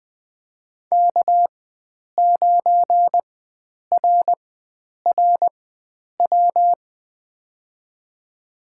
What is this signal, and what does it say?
Morse code, heard as a keyed tone, about 20 wpm, spaced as for 9 wpm, 700 Hz. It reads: K9RRW
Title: USA Call Signs - 5 Characters - 4x Character Spacing - 20wpm